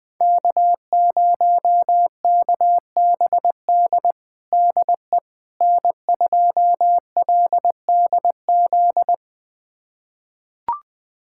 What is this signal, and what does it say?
Morse code, heard as a keyed tone, about 21 wpm, 700 Hz. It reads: K0KBD DE N2LDZ
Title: USA Callsign DE Callsign - 5 Characters - 20wpm